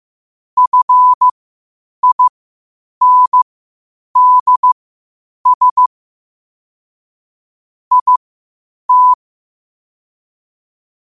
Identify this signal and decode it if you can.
Morse code, heard as a keyed tone, about 15 wpm, spaced as for 8 wpm, 1000 Hz. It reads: FINDS IT
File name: qotdfiles / 8wpm